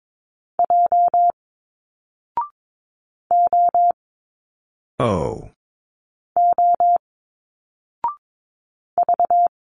Morse code, heard as a keyed tone, about 22 wpm, 700 Hz. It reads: J O O V
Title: Single Letters 22wpm